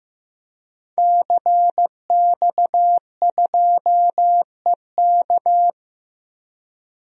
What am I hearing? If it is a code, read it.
Morse code, heard as a keyed tone, about 15 wpm, 700 Hz. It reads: CX2EK
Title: DXCC-100 Call Signs - 5 Characters - 15wpm